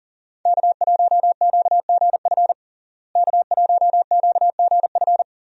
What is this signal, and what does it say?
Morse code, heard as a keyed tone, about 40 wpm, 700 Hz. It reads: K1QGF K1QGF
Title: USA Call Signs - 5 Characters - Repeated 3x - 40wpm